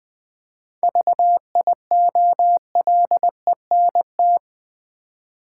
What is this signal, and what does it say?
Morse code, heard as a keyed tone, about 20 wpm, 700 Hz. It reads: VIOLENT
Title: 7 Letter Words - First 3 Letters Given - 20wpm